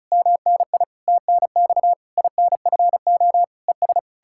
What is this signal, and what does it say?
Morse code, heard as a keyed tone, about 46 wpm, 700 Hz. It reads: MNI TNX INFO ES